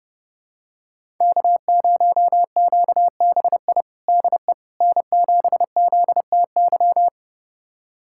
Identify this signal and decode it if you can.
Morse code, heard as a keyed tone, about 30 wpm, 700 Hz. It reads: K0QBI DE N7ZTY
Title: USA Callsign DE Callsign - 5 Characters - 30wpm